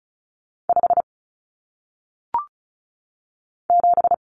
Morse code, heard as a keyed tone, about 36 wpm, 700 Hz. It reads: 5 7
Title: Common Cut Numbers - 1-Digit 35wpm